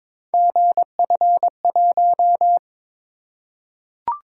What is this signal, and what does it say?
Morse code, heard as a keyed tone, about 21 wpm, 700 Hz. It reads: GF1
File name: practice / 22wpm / Three Letter-Number Combinations - 22wpm